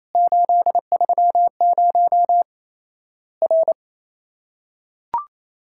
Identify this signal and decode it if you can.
Morse code, heard as a keyed tone, about 28 wpm, 700 Hz. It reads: 830 R
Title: Worked All Europe DX Contest - QTC Exchange - 28wpm